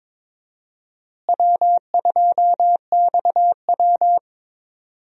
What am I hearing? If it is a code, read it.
Morse code, heard as a keyed tone, about 22 wpm, 700 Hz. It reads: W2XW
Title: USA Call Signs - 4 Characters - Repeated 3x - 22wpm